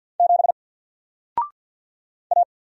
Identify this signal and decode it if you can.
Morse code, heard as a keyed tone, about 52 wpm, 700 Hz. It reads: 7 A
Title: Single Letter-Number 50wpm